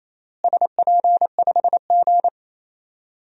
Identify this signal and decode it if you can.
Morse code, heard as a keyed tone, about 28 wpm, 700 Hz. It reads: SP5G
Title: DXCC-100 Call Signs - 4 Characters - Repeated 3x - 28wpm